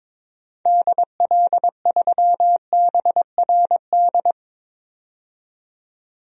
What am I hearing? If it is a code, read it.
Morse code, heard as a keyed tone, about 22 wpm, 700 Hz. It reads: DL3BRD
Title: DXCC-100 Call Signs - 6 Characters - 22wpm